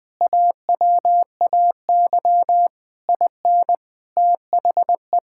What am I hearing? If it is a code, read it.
Morse code, heard as a keyed tone, about 20 wpm, 700 Hz. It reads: AWAY IN THE
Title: Sentences from Top 1000 Words - Review Part 5 - 20wpm